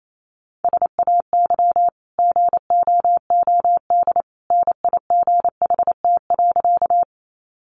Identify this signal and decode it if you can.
Morse code, heard as a keyed tone, about 28 wpm, 700 Hz. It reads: SAY GOOD NIGHT.
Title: Sentences from Top 300 Words 28wpm